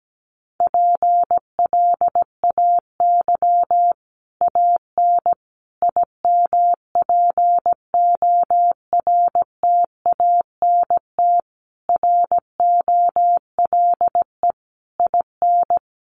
Morse code, heard as a keyed tone, about 17 wpm, 700 Hz. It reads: PLAY AN IMPORTANT ROLE IN
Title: Sets of 5 Words Encore 17wpm